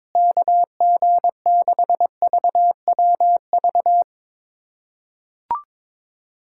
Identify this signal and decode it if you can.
Morse code, heard as a keyed tone, about 22 wpm, 700 Hz. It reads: KG6VWV
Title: USA Callsign DE Callsign - 6 Characters - 22wpm